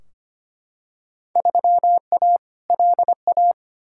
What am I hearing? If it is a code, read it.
Morse code, heard as a keyed tone, about 25 wpm, 700 Hz. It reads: 3A LA
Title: Field Day Exchanges - 25wpm